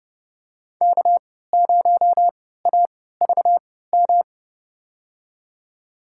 Morse code, heard as a keyed tone, about 32 wpm, spaced as for 17 wpm, 700 Hz. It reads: K0AVM
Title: USA Call Signs - 5 Characters - 3x Character Spacing - 30wpm